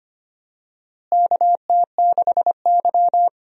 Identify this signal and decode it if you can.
Morse code, heard as a keyed tone, about 25 wpm, 700 Hz. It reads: KT6Y
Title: USA Call Signs - 4 Characters - Repeated 3x - 25wpm